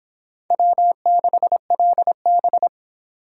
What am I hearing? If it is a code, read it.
Morse code, heard as a keyed tone, about 31 wpm, 700 Hz. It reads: W6LB